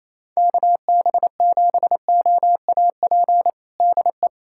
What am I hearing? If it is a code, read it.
Morse code, heard as a keyed tone, about 28 wpm, 700 Hz. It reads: KB7OAP DE
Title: USA Callsign DE Callsign - 6 Characters - 28wpm